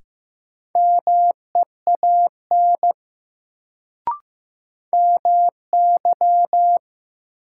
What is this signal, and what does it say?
Morse code, heard as a keyed tone, about 15 wpm, 700 Hz. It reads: MEAN MY
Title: Sentences from Top 200 Words - New Word Drill - 15wpm